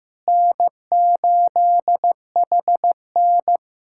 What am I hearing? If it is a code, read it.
Morse code, heard as a keyed tone, about 15 wpm, 700 Hz. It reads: N8HN
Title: USA Callsign DE Callsign - 4 Characters - 15wpm